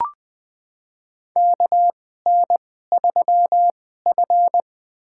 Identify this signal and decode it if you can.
Morse code, heard as a keyed tone, about 20 wpm, spaced as for 14 wpm, 700 Hz. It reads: KN3F
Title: USA Call Signs - 4 Characters - 2x Character Spacing - 20wpm